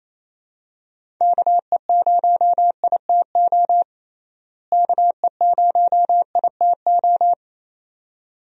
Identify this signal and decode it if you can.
Morse code, heard as a keyed tone, about 28 wpm, 700 Hz. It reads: KE0ITO KE0ITO
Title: USA Call Signs - 6 Characters - Repeated 2x - 28wpm